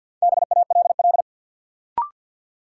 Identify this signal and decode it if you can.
Morse code, heard as a keyed tone, about 52 wpm, 700 Hz. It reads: BALL
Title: Sentences from Top 900 Words - New Word Drill - 50wpm